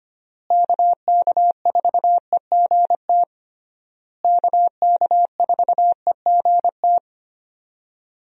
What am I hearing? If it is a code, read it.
Morse code, heard as a keyed tone, about 25 wpm, 700 Hz. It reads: KK4EGT KK4EGT
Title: USA Call Signs - 6 Characters - Repeated 3x - 25wpm